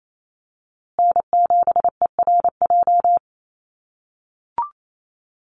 Morse code, heard as a keyed tone, about 28 wpm, 700 Hz. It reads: N7ERJ
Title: USA Call Signs Repeated 2x 28wpm